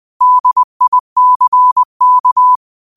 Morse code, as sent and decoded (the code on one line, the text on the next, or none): -.. .. -.-. -.-
DICK